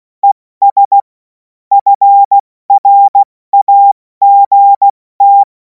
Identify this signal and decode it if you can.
Morse code, heard as a keyed tone, about 16 wpm, spaced as for 14 wpm, 800 Hz. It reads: ES FRAGT
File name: Morsekurs / Witze / Witz 03. März 2026 70 F80